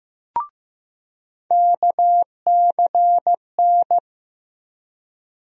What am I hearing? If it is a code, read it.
Morse code, heard as a keyed tone, about 15 wpm, 700 Hz. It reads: KCN
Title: USA Call Sign Postfixes - 3 Letters 15wpm